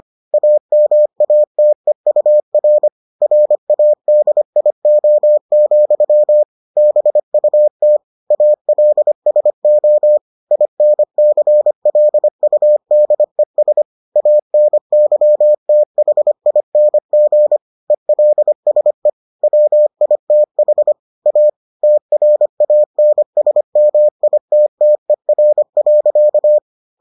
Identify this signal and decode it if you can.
Morse code, heard as a keyed tone, about 25 wpm, 600 Hz. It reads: AMATEUR RADIO, BUT ALSO INCLUDES ANYTHING ELSE WITH A TRANSMITTER.